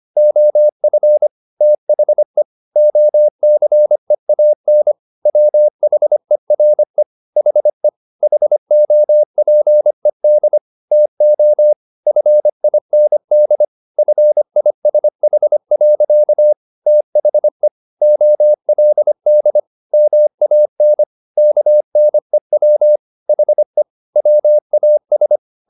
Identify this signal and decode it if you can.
Morse code, heard as a keyed tone, about 25 wpm, 600 Hz. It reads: OF THE OCEAN WHERE HE HOPED TO FIND FISH. THE OLD MAN KNEW HE WAS
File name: the old man and the sea part 1 25wpm